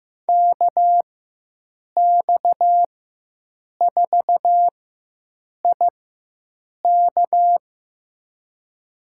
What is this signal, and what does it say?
Morse code, heard as a keyed tone, about 15 wpm, spaced as for 7 wpm, 700 Hz. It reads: KX4IK
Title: USA Call Signs - 4x Character Spacing - 15wpm